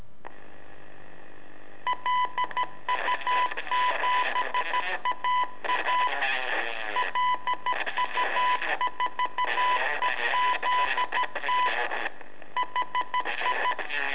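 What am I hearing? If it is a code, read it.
Morse code, heard as a keyed tone, about 13 wpm, 1000 Hz, with noise. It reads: LU7AA LU4EZT HI
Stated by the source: Telemetry in Morse from LU7AA and LU4EZ, AMSAT and Turma Mar del Plata, sent from a balloon.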